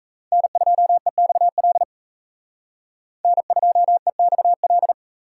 Morse code, heard as a keyed tone, about 36 wpm, 700 Hz. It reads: N2EXL N2EXL